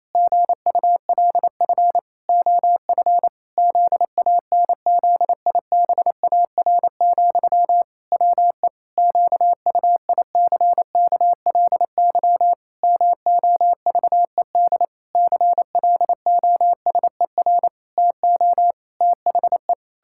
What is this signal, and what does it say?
Morse code, heard as a keyed tone, about 28 wpm, 700 Hz. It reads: GULF OF ZANZIBAR, WE QUICKLY MOVED CLOSER TO THE
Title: Pangram Challenge v2 - 28wpm